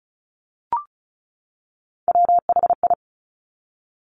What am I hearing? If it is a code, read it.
Morse code, heard as a keyed tone, about 36 wpm, 690 Hz. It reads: WHI
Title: Top Three Letter Combinations in Words 35wpm